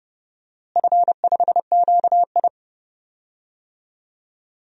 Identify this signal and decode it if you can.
Morse code, heard as a keyed tone, about 32 wpm, 700 Hz. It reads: F5QI
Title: DXCC-100 Call Signs - 4 Characters - Repeated 3x - 30wpm